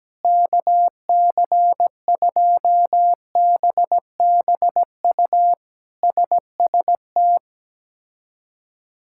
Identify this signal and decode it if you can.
Morse code, heard as a keyed tone, about 17 wpm, 700 Hz. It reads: KC2BBU SST
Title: K1USN SST QSO Practice - 17wpm